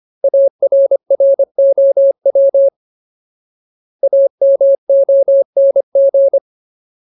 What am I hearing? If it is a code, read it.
Morse code, heard as a keyed tone, about 25 wpm, 550 Hz. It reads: ARROW AMONG
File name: Ultimate Word List / 25WPM Ultimate